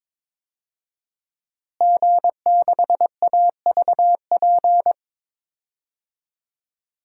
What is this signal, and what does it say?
Morse code, heard as a keyed tone, about 22 wpm, 700 Hz. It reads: G6AVP
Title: DXCC-100 Call Signs - 5 Characters - Repeated 3x - 22wpm